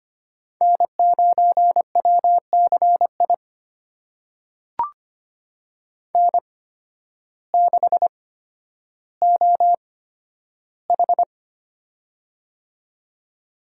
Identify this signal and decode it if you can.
Morse code, heard as a keyed tone, about 25 wpm, spaced as for 12 wpm, 700 Hz. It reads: N9WCI N6OH
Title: USA Call Signs - 8x Character Spacing - 25wpm